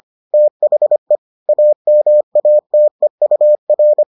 Morse code, heard as a keyed tone, about 25 wpm, 600 Hz. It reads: THE AMATEUR